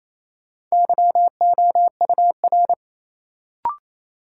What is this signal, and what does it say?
Morse code, heard as a keyed tone, about 28 wpm, 700 Hz. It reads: YOUR
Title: Sets of 2 Words - New Word Drill - 28wpm